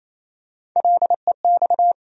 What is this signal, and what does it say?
Morse code, heard as a keyed tone, about 28 wpm, 700 Hz. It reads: LEX
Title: USA Call Sign Postfixes - 3 Letters 28wpm